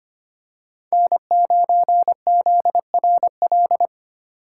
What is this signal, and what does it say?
Morse code, heard as a keyed tone, about 25 wpm, 700 Hz. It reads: N9ZRL
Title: USA Call Signs - 5 Characters - 2x Character Spacing - 25wpm